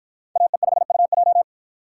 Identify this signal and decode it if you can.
Morse code, heard as a keyed tone, about 53 wpm, 700 Hz. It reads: AE5UJ